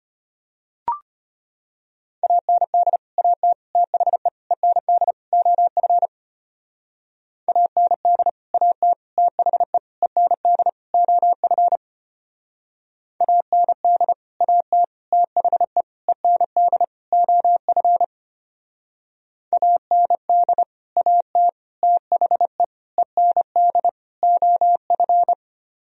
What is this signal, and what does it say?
Morse code, heard as a keyed tone, about 32 wpm, 700 Hz. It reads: AND AT THE END OF AND AT THE END OF AND AT THE END OF AND AT THE END OF